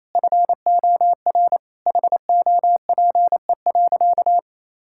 Morse code, heard as a keyed tone, about 28 wpm, 700 Hz. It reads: FOR HOPE.
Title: Sentences from Top 1000 Words - Review Part 5 - 28wpm